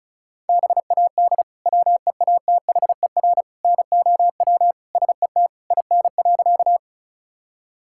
Morse code, heard as a keyed tone, about 34 wpm, 690 Hz. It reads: BAD WEATHER NOW SET IN.
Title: Sentences from Top 1000 Words - Review Part 5 - 35wpm